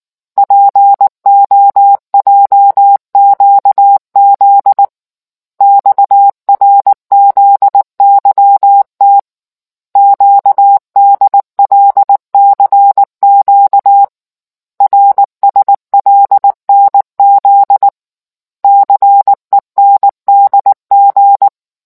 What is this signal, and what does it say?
Morse code, heard as a keyed tone, about 19 wpm, 800 Hz. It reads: POJQZ XRZYT QDLCQ RSLNZ CENDG